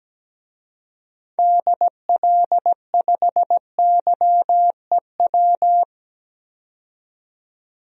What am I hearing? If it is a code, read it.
Morse code, heard as a keyed tone, about 17 wpm, 700 Hz. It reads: DL5YEW
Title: DXCC-100 Call Signs - 6 Characters - 17wpm